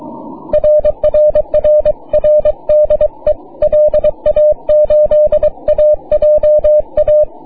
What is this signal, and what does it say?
Morse code, heard as a keyed tone, about 22 wpm, 600 Hz. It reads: RRRR DE LA8AJA
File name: wp / ct3hf 6m cw